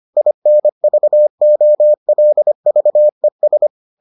Morse code, heard as a keyed tone, about 25 wpm, 600 Hz. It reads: INVOLVES